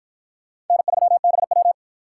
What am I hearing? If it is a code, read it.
Morse code, heard as a keyed tone, about 46 wpm, 700 Hz. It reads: N3BW